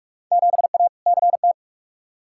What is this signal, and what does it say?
Morse code, heard as a keyed tone, about 41 wpm, 700 Hz. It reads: 7A CT
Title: Field Day Exchanges - 45wpm